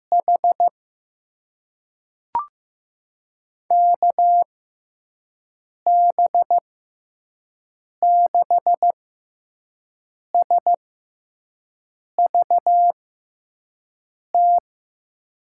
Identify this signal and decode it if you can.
Morse code, heard as a keyed tone, about 15 wpm, spaced as for 5 wpm, 700 Hz. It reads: H KB6SVT
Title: USA Call Signs - 6 Characters - 6x Character Spacing - 15wpm